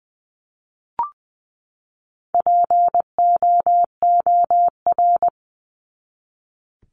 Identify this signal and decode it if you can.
Morse code, heard as a keyed tone, about 20 wpm, 700 Hz. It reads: POOR
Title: Top 751-1000 Words 20wpm